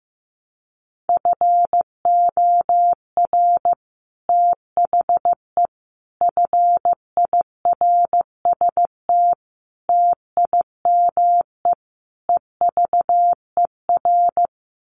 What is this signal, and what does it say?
Morse code, heard as a keyed tone, about 15 wpm, 700 Hz. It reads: FOR THE FIRST TIME EVER
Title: Sets of 5 Words Encore2 15wpm